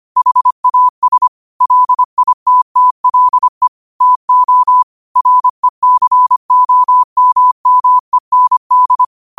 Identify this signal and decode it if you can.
Morse code, heard as a keyed tone, about 25 wpm, 1000 Hz. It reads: HAS LITTLE TO RECOMMEND